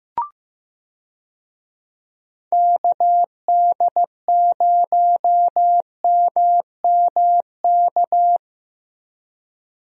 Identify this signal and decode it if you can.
Morse code, heard as a keyed tone, about 15 wpm, 700 Hz. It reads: KD0MMK
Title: USA Call Signs - 6 Characters - Repeated 3x - 15wpm